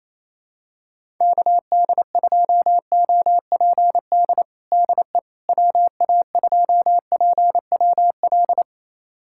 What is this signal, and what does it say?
Morse code, heard as a keyed tone, about 28 wpm, 700 Hz. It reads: KD2OPD DE WA2PWL
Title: USA Callsign DE Callsign 28wpm